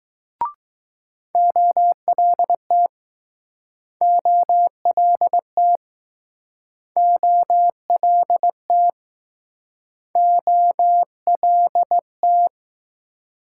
Morse code, heard as a keyed tone, about 20 wpm, 700 Hz. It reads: OLT OLT OLT OLT